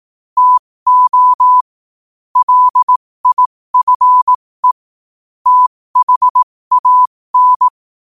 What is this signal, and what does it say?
Morse code, heard as a keyed tone, about 18 wpm, spaced as for 15 wpm, 1000 Hz. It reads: TO LIFE THAN
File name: qotdfiles / 15wpm